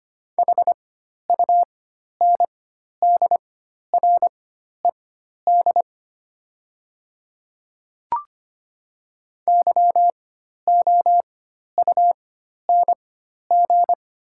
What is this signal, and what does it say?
Morse code, heard as a keyed tone, about 25 wpm, spaced as for 12 wpm, 700 Hz. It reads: HUNDRED YOUNG
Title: 4x Character Spacing - Top 500 Words - 25wpm